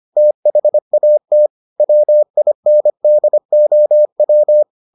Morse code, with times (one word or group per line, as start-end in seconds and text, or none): 0.2-1.5 THAT
1.8-4.6 WINDOW